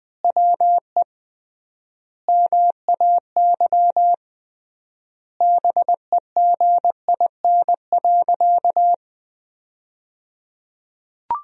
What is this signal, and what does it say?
Morse code, heard as a keyed tone, about 20 wpm, 700 Hz. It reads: WE MAY BEGIN.
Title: 3x Word-Spacing - Sentences from Top 300 Words - 20wpm